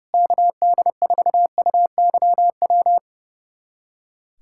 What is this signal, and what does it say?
Morse code, heard as a keyed tone, about 30 wpm, 700 Hz. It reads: KD4UYW
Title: USA Callsign DE Callsign - 6 Characters - 30wpm